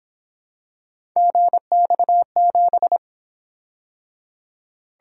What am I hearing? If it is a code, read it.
Morse code, heard as a keyed tone, about 27 wpm, 700 Hz. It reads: GX7
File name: Speed-Racing - Three Letter-Number Combinations - 22wpm